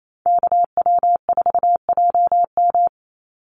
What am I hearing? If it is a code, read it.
Morse code, heard as a keyed tone, about 28 wpm, 700 Hz. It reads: KW4JM
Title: NA SOTA Activator Callsigns 28wpm